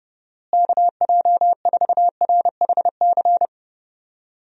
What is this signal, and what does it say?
Morse code, heard as a keyed tone, about 31 wpm, 700 Hz. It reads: KJ4RHC
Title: USA Call Signs - 6 Characters - Repeated 2x - 30wpm